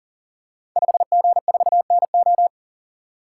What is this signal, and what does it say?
Morse code, heard as a keyed tone, about 39 wpm, 700 Hz. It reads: 5G4NO
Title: Code Groups - 5 Letter:Numbers - 40wpm